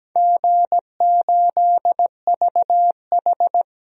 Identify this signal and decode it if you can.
Morse code, heard as a keyed tone, about 17 wpm, 700 Hz. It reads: G8VH
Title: DXCC-100 Call Signs - 4 Characters - 17wpm